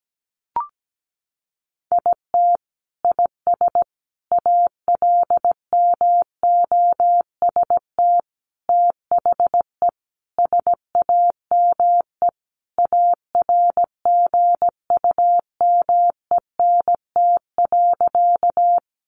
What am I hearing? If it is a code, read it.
Morse code, heard as a keyed tone, about 17 wpm, 700 Hz. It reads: IT IS ALMOST THE SAME ARGUMENT.
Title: Sentences from Top 1400 Words 17wpm